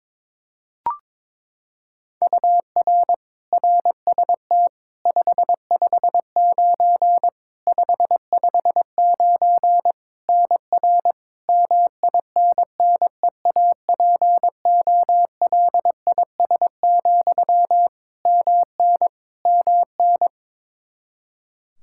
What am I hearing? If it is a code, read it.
Morse code, heard as a keyed tone, about 22 wpm, 700 Hz. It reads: UR RST 559 559 NR MINNEAPOLIS, MN MN
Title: Practice QSO - Exchange 1 - Beginning - 22wpm